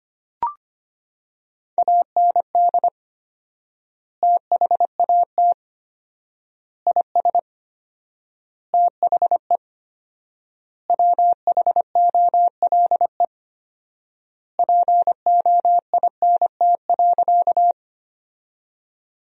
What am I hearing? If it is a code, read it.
Morse code, heard as a keyed tone, about 25 wpm, 700 Hz. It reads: AND THAT IS THE WHOLE POINT.
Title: 4x Word-Spacing - Sentences from Top 400 Words - 25wpm